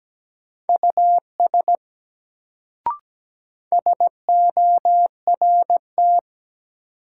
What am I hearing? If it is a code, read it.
Morse code, heard as a keyed tone, about 17 wpm, 700 Hz. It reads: US SORT
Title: Sets of 2 Words - New Word Drill - 17wpm